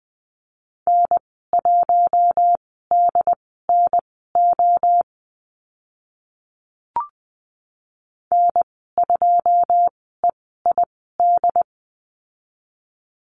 Morse code, heard as a keyed tone, about 20 wpm, spaced as for 14 wpm, 700 Hz. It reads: N1DNO N2EID
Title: 088 Lesson 15g - Number 5 - Call Signs 20wpm